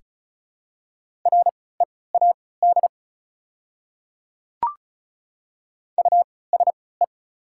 Morse code, heard as a keyed tone, about 34 wpm, spaced as for 20 wpm, 700 Hz. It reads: READ USE
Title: 3x Character Spacing - Top 200 Words - 35wpm